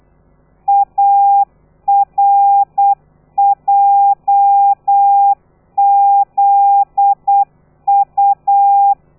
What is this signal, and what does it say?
Morse code, heard as a keyed tone, about 8 wpm, 800 Hz. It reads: ARJZU
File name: telegrafia / i2tfj / Lez13 35